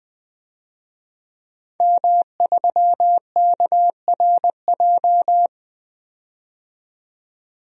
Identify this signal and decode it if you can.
Morse code, heard as a keyed tone, about 20 wpm, 700 Hz. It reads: M3KRJ
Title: DXCC-100 Call Signs - 5 Characters - 20wpm